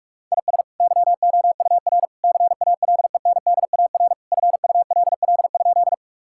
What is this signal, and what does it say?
Morse code, heard as a keyed tone, about 47 wpm, 700 Hz. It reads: IS YOUR CALENDAR FULL?
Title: Sentences from Top 1600 Words 45wpm